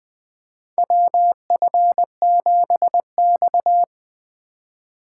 Morse code, heard as a keyed tone, about 20 wpm, 700 Hz. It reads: WF7X
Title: USA Call Signs - 4 Characters - 3x Character Spacing - 20wpm